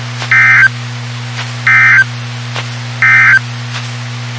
Emergency Alert System (SAME) tones, heard as harsh buzzing bursts. It is heard from 0.2 to 3.5 s.
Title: WXK91 NOAA Weather Radio